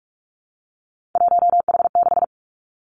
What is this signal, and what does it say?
Morse code, heard as a keyed tone, about 46 wpm, 700 Hz. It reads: 1H6
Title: Three Challenging Letter-Number Combinations 45wpm